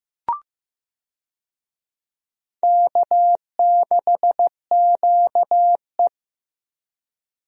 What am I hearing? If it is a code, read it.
Morse code, heard as a keyed tone, about 15 wpm, 700 Hz. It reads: K6QE
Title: USA Call Signs - 4 Characters - Repeated 3x - 15wpm